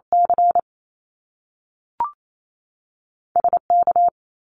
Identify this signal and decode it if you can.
Morse code, heard as a keyed tone, about 28 wpm, spaced as for 6 wpm, 700 Hz. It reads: CSK
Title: Canadian Provinces and Territories 28wpm